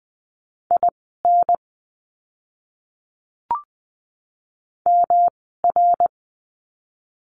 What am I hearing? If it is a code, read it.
Morse code, heard as a keyed tone, about 20 wpm, spaced as for 15 wpm, 700 Hz. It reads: IN MR
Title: 244 Lesson 39f - Letter z - Words 20wpm